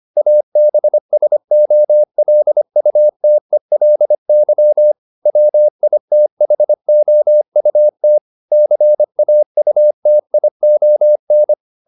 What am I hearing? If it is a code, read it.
Morse code, heard as a keyed tone, about 25 wpm, 600 Hz. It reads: ABSOLUTELY WITHOUT CAUTION